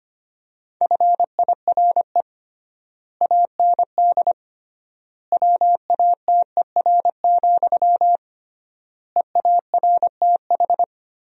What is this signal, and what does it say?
Morse code, heard as a keyed tone, about 25 wpm, 700 Hz. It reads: FIRE AND WATER, EARTH
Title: 3x Word-Spacing - Sentences from Top 400 Words - 25wpm